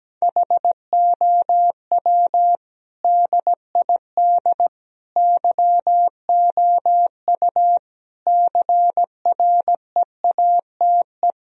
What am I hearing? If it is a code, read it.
Morse code, heard as a keyed tone, about 17 wpm, 700 Hz. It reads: HOW DID YOU CREATE
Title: Sentences from Top 1000 Words - Review Part 3 - 17wpm